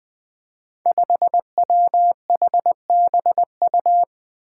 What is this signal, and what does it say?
Morse code, heard as a keyed tone, about 20 wpm, 700 Hz. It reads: 5WHBU
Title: Five Challenging Letter-Number Combinations 20wpm